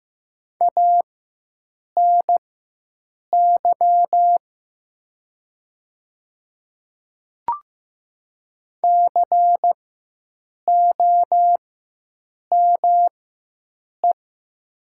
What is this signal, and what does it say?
Morse code, heard as a keyed tone, about 15 wpm, spaced as for 7 wpm, 700 Hz. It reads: ANY COME
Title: 4x Character Spacing - Top 300 Words - 15wpm